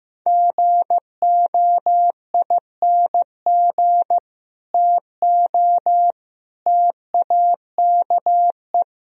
Morse code, heard as a keyed tone, about 16 wpm, 700 Hz. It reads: GOING TO TAKE
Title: Speed-Racing - Sets of 3 Words 15wpm